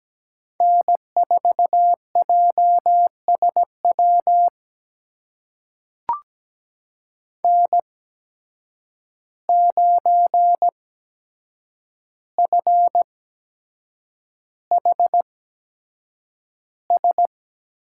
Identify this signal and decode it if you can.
Morse code, heard as a keyed tone, about 17 wpm, spaced as for 7 wpm, 700 Hz. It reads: N4JSW N9FHS
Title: USA Call Signs - 5 Characters - 8x Character Spacing - 17wpm